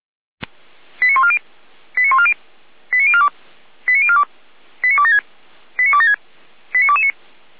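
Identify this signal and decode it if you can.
Five-tone selective call (ZVEI), heard as quick runs of five tones, tones of about 70 ms. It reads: 89140 89140 89042 89042 89276 89276 89229
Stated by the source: ZVEI five-tone selective call received on SDR, recorded by Feuerwehr Orlamünde.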